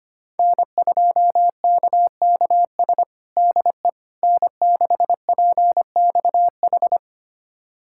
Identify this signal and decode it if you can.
Morse code, heard as a keyed tone, about 25 wpm, 700 Hz. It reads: N2KKS DE N6PXH
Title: USA Callsign DE Callsign - 5 Characters - 25wpm